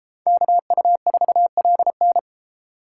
Morse code, heard as a keyed tone, about 32 wpm, 700 Hz. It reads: KU4LN